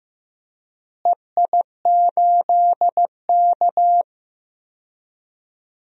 Morse code, heard as a keyed tone, about 15 wpm, 700 Hz. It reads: EI8K
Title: DXCC-100 Call Signs - 4 Characters - Repeated 3x - 15wpm